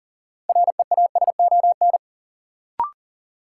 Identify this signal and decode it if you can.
Morse code, heard as a keyed tone, about 40 wpm, 700 Hz. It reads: REASON
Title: Sets of 5 Words Encore - New Word Drill - 40wpm